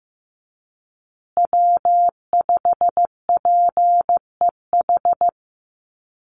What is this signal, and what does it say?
Morse code, heard as a keyed tone, about 15 wpm, 700 Hz. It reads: W5PEH
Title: Call Signs Repeated 2x - 15wpm